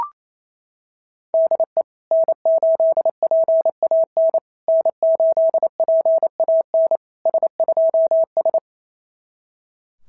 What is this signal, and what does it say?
Morse code, heard as a keyed tone, about 27 wpm, 660 Hz. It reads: DE N8PAN N8PAN S2S
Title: SOTA QSO Practice 28wpm